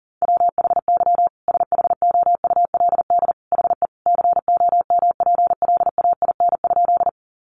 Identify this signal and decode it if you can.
Morse code, heard as a keyed tone, about 37 wpm, 700 Hz. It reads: WHY SHOULD HE COMPLAIN?
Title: Sentences from Top 1300 Words 40wpm